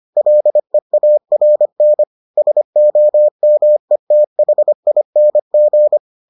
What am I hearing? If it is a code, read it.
Morse code, heard as a keyed tone, about 25 wpm, 600 Hz. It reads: LEARN SOMETHING